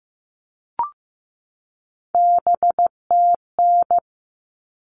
Morse code, heard as a keyed tone, about 15 wpm, 700 Hz. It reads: BTN
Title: All Cut Numbers - 3-Digit 15wpm